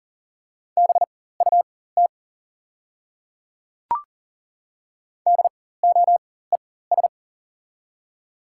Morse code, heard as a keyed tone, about 38 wpm, spaced as for 18 wpm, 700 Hz. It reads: BUT DOES
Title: 4x Character Spacing - Top 300 Words - 40wpm